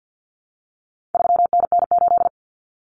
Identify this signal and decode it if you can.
Morse code, heard as a keyed tone, about 49 wpm, 700 Hz. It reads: 3NN8
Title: Common Cut Numbers - 4-Digit 50wpm